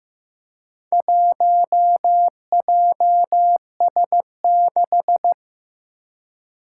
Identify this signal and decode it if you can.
Morse code, heard as a keyed tone, about 15 wpm, 700 Hz. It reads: 1JS6
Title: Four Challenging Letter-Number Combinations 15wpm